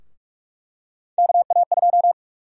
Morse code, heard as a keyed tone, about 46 wpm, 700 Hz. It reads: KA2
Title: USA Call Sign Prefixes - 45wpm